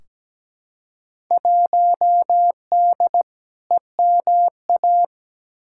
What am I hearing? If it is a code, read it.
Morse code, heard as a keyed tone, about 17 wpm, 700 Hz. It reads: 1D EMA
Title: Field Day Exchanges - 17wpm